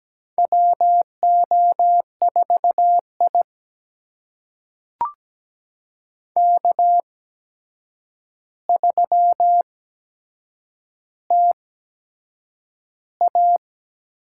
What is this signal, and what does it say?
Morse code, heard as a keyed tone, about 17 wpm, spaced as for 7 wpm, 700 Hz. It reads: WO4I K3TA
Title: USA Call Signs - 4 Characters - 8x Character Spacing - 17wpm